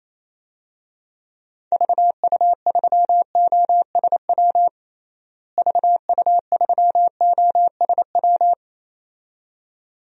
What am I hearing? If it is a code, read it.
Morse code, heard as a keyed tone, about 28 wpm, 700 Hz. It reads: VU3OSW VU3OSW
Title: DXCC-100 Call Signs - 6 Characters - Repeated 2x - 28wpm